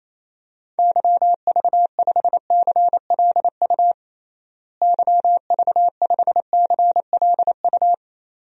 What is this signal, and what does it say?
Morse code, heard as a keyed tone, about 28 wpm, 700 Hz. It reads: YV5CLU YV5CLU
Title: DXCC-100 Call Signs - 6 Characters - Repeated 2x - 28wpm